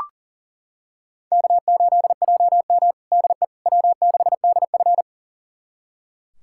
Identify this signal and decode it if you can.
Morse code, heard as a keyed tone, about 40 wpm, 700 Hz. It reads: K8JM DE W6DF
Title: USA Callsign DE Callsign - 4 Characters - 40wpm